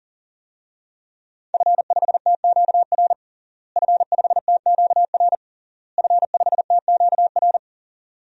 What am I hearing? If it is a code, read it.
Morse code, heard as a keyed tone, about 40 wpm, 700 Hz. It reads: F5TQR F5TQR F5TQR
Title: DXCC-100 Call Signs - 5 Characters - Repeated 3x - 40wpm